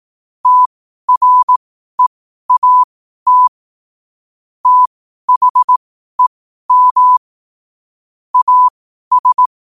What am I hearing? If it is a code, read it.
Morse code, heard as a keyed tone, about 18 wpm, spaced as for 12 wpm, 1000 Hz. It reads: TREAT THEM AS
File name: qotdfiles / 12wpm